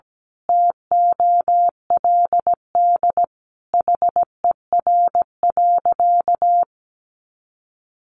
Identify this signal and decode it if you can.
Morse code, heard as a keyed tone, about 17 wpm, 700 Hz. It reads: TOLD HER.
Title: Sentences from Top 1200 Words 17wpm